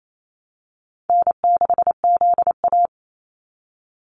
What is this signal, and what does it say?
Morse code, heard as a keyed tone, about 28 wpm, 700 Hz. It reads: N6ZA
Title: NA SOTA Activator Callsigns 28wpm